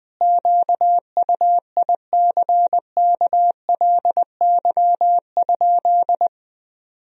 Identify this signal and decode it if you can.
Morse code, heard as a keyed tone, about 20 wpm, 700 Hz. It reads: QUICKLY?
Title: Sentences from Top 1000 Words - Review Part 3 - 20wpm